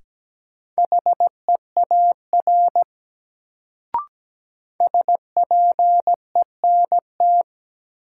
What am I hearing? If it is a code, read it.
Morse code, heard as a keyed tone, about 17 wpm, 700 Hz. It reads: HEAR SPENT
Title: Sets of 5 Words - New Word Drill - 17wpm